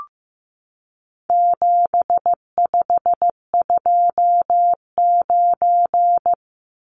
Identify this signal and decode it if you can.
Morse code, heard as a keyed tone, about 15 wpm, 700 Hz. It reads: 7529
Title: Numbers - 4-Digit 15wpm